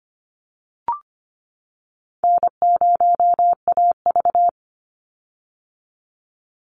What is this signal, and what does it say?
Morse code, heard as a keyed tone, about 25 wpm, 700 Hz. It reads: N0AV
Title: Easy Call Signs - 25wpm